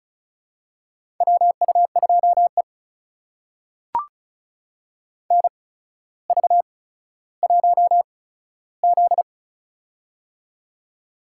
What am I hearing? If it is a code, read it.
Morse code, heard as a keyed tone, about 34 wpm, spaced as for 15 wpm, 700 Hz. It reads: WU2E NV1Z
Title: USA Call Signs - 4 Characters - 8x Character Spacing - 35wpm